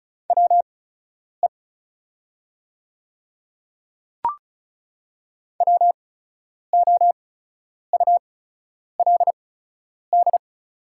Morse code, heard as a keyed tone, about 35 wpm, spaced as for 10 wpm, 690 Hz. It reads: WE WOULD
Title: 8x Character Spacing - Top 100 Words - 35wpm